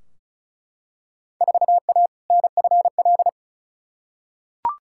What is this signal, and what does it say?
Morse code, heard as a keyed tone, about 34 wpm, 700 Hz. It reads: 4A NFL
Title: Field Day Exchanges - 35wpm